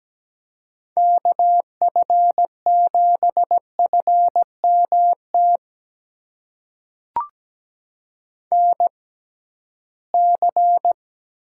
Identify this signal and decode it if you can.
Morse code, heard as a keyed tone, about 17 wpm, spaced as for 13 wpm, 700 Hz. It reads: KF7FMT NC
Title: USA Call Signs - 6x Character Spacing - 17wpm